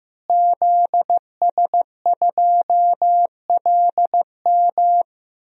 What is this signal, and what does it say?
Morse code, heard as a keyed tone, about 15 wpm, 700 Hz. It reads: ZS2LM
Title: DXCC-100 Call Signs - 5 Characters - Repeated 3x - 15wpm